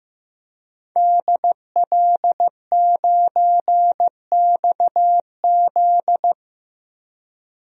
Morse code, heard as a keyed tone, about 15 wpm, 700 Hz. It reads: DL9XZ
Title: DXCC-100 Call Signs - 5 Characters - 15wpm